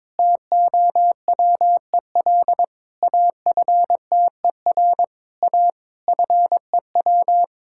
Morse code, heard as a keyed tone, about 22 wpm, 690 Hz. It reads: TOWEL AFTER A FEW
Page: morse-code-ninja-practice-22wpm